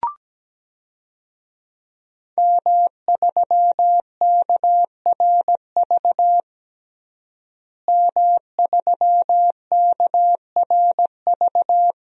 Morse code, heard as a keyed tone, about 17 wpm, 700 Hz. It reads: M3KRV M3KRV
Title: DXCC-100 Call Signs - 5 Characters - Repeated 2x - 17wpm